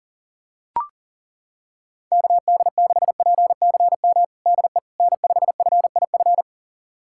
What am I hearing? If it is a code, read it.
Morse code, heard as a keyed tone, about 40 wpm, 700 Hz. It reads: KD6PCM DE N5FIF
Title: USA Callsign DE Callsign 40wpm